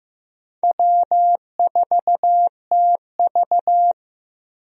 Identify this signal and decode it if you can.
Morse code, heard as a keyed tone, about 15 wpm, 700 Hz. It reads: W4TV
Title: USA Call Signs - 4 Characters - Repeated 2x - 15wpm